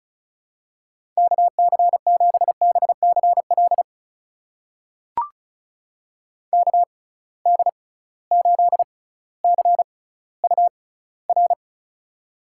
Morse code, heard as a keyed tone, about 36 wpm, spaced as for 18 wpm, 700 Hz. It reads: KC7BCL KD8CUR
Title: USA Call Signs - 6 Characters - 6x Character Spacing - 35wpm